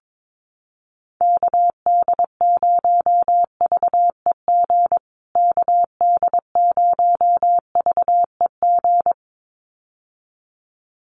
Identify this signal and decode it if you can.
Morse code, heard as a keyed tone, about 22 wpm, 700 Hz. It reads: KD0VEG KD0VEG
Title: USA Call Signs Repeated 2x 22wpm